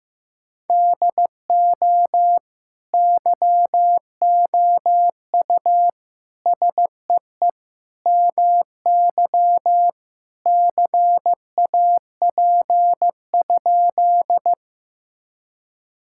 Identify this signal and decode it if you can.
Morse code, heard as a keyed tone, about 16 wpm, 700 Hz. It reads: DO YOU SEE MY CAP?
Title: Sentences from Top 1600 Words 15wpm